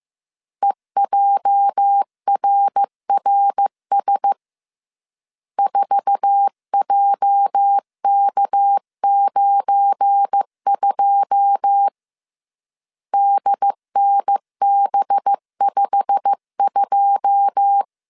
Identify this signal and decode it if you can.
Morse code, heard as a keyed tone, about 15 wpm, 800 Hz. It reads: EJRRS 4JK92 DNB52
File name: CW / DET / 1 9 1500C S C 14wpm 3245